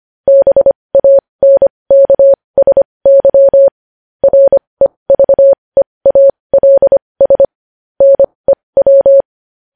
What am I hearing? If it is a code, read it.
Morse code, heard as a keyed tone, about 25 wpm, spaced as for 20 wpm, 550 Hz. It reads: BANKSY REVEALS NEW
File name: podcast / News headlines-1x-25x20